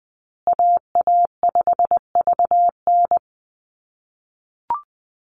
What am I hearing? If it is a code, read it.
Morse code, heard as a keyed tone, about 20 wpm, 700 Hz. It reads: AA5VN
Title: USA Call Signs Repeated 3x 20wpm